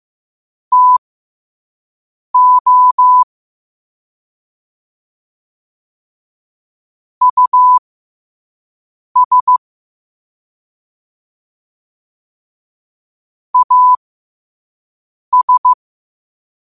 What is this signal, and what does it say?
Morse code, heard as a keyed tone, about 15 wpm, spaced as for 5 wpm, 1000 Hz. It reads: TO US AS
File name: qotdfiles / 5wpm